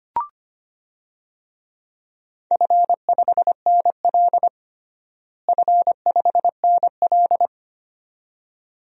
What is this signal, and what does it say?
Morse code, heard as a keyed tone, about 25 wpm, 700 Hz. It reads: F5NL F5NL
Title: DXCC-100 Call Signs - 4 Characters - Repeated 2x - 25wpm